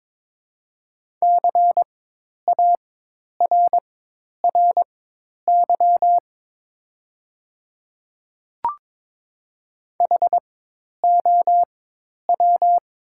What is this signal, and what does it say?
Morse code, heard as a keyed tone, about 22 wpm, spaced as for 10 wpm, 700 Hz. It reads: CARRY HOW
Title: 4x Character Spacing - Top 300 Words - 22wpm